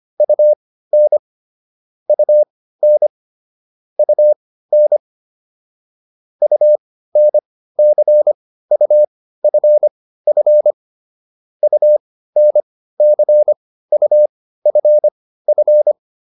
Morse code, heard as a keyed tone, about 25 wpm, spaced as for 15 wpm, 600 Hz. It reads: UN UN UN UNCUFF UNCUFF